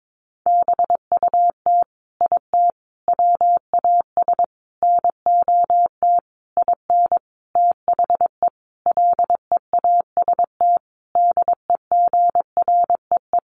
Morse code, heard as a keyed tone, about 22 wpm, 700 Hz. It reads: BUT IT WAS NOT IN THE LEAST DEGREE
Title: Sentences from Top 1500 Words 22wpm